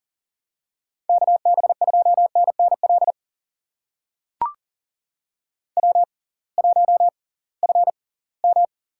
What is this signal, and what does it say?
Morse code, heard as a keyed tone, about 38 wpm, spaced as for 23 wpm, 700 Hz. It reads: KB2NNL W1FM
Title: USA Call Signs - 6x Character Spacing - 40wpm